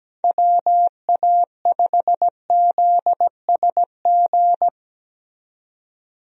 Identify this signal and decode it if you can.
Morse code, heard as a keyed tone, about 17 wpm, 700 Hz. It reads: WA5ZSG
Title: USA Call Signs - 6 Characters - 17wpm